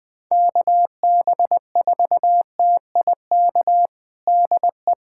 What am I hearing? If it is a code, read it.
Morse code, heard as a keyed tone, about 20 wpm, 700 Hz. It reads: KB4TIK DE
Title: USA Callsign DE Callsign - 6 Characters - 20wpm